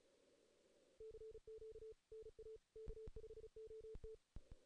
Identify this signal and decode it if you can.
Morse code, heard as a keyed tone, about 35 wpm, 440 Hz. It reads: CQ NA K5OT
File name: 2016 JAN CW NAQP / 2016 Jan naqp k5ot - 0-2